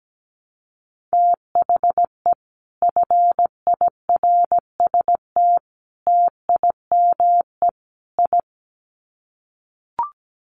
Morse code, heard as a keyed tone, about 17 wpm, 700 Hz. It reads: THE FIRST TIME I
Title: Sets of 4 Words 17wpm